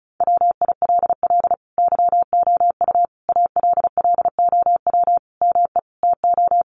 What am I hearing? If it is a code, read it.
Morse code, heard as a keyed tone, about 34 wpm, 700 Hz. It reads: WILL YOU ALLOW ME TO
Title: Sentences from Top 1200 Words 35wpm